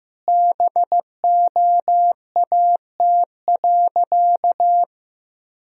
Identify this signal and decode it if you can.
Morse code, heard as a keyed tone, about 16 wpm, 700 Hz. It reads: BOAT.
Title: Sentences from Top 1000 Words - Review Part 3 - 15wpm